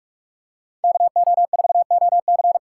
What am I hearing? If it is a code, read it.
Morse code, heard as a keyed tone, about 46 wpm, 700 Hz. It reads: KO4OC
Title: USA Call Signs - 5 Characters - 4x Character Spacing - 45wpm